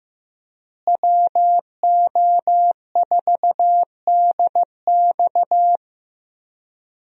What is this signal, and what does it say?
Morse code, heard as a keyed tone, about 15 wpm, 700 Hz. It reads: WO4DX
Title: USA Call Signs - 5 Characters - Repeated 2x - 15wpm